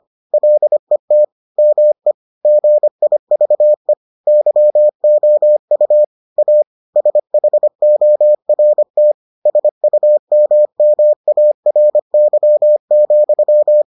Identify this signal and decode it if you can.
Morse code, heard as a keyed tone, about 25 wpm, 600 Hz. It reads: LET ME GIVE YOU A SHORT SUMMARY,